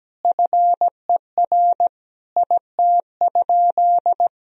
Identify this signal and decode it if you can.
Morse code, heard as a keyed tone, about 17 wpm, 700 Hz. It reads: FER IT?
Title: Sentences from Top 400 Words and Abbreviations 17wpm